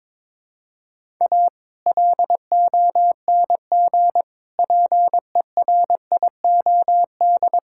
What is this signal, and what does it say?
Morse code, heard as a keyed tone, about 22 wpm, 700 Hz. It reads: A LONG PERIOD
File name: practice / 22wpm / Build A Phrase - 5-Word Phrases - 22wpm